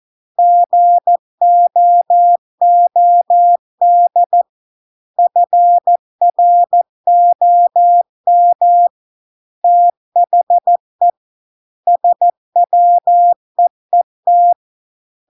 Morse code, heard as a keyed tone, about 14 wpm, 700 Hz. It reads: GOOD FROM THE SWEET